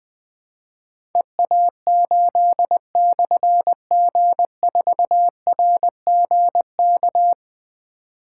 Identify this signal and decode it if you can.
Morse code, heard as a keyed tone, about 20 wpm, 700 Hz. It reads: EA8/G4RGK
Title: Difficult Call Signs 20wpm